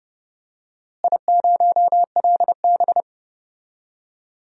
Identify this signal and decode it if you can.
Morse code, heard as a keyed tone, about 30 wpm, 700 Hz. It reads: I0LB
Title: DXCC-100 Call Signs - 4 Characters - Repeated 3x - 30wpm